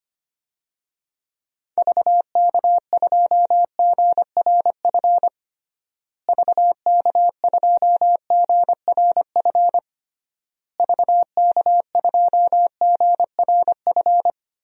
Morse code, heard as a keyed tone, about 25 wpm, 700 Hz. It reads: VK2GRF VK2GRF VK2GRF
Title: DXCC-100 Call Signs - 6 Characters - Repeated 3x - 25wpm